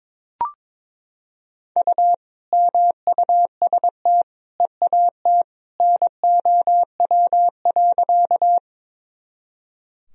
Morse code, heard as a keyed tone, about 22 wpm, 700 Hz. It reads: U MUST EAT NOW.
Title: Sentences from Top 300 Words and Abbreviations 22wpm